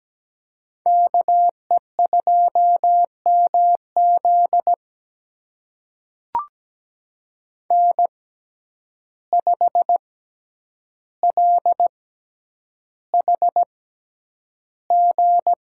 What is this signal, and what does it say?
Morse code, heard as a keyed tone, about 17 wpm, spaced as for 9 wpm, 700 Hz. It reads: KE2MZ N5LHG
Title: USA Call Signs - 5 Characters - 6x Character Spacing - 17wpm